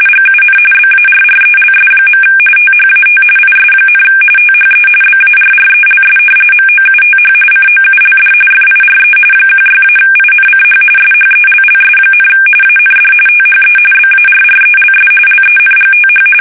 RTTY, about 45.45 baud. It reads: GUN BARRELS. THAT IS, THE BARRELS MOUTH COULD FIRE 16-INCH SHELLS, AND THE BARRELS LENGTH
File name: fsk850 / battleshipnj